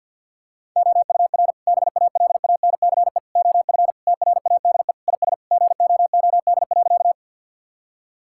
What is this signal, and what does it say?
Morse code, heard as a keyed tone, about 47 wpm, 700 Hz. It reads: OUR BALANCE OF TRADE IS GOOD.
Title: Sentences from Top 1000 Words - Review Part 6 - 50wpm